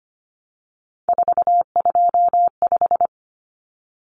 Morse code, heard as a keyed tone, about 25 wpm, 700 Hz. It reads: 425
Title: Numbers - 3-Digit 25wpm